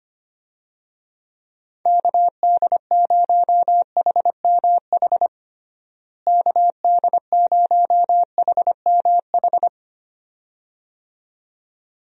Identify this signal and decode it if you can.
Morse code, heard as a keyed tone, about 25 wpm, 700 Hz. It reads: KD0HMH KD0HMH
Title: USA Call Signs - 6 Characters - Repeated 2x - 25wpm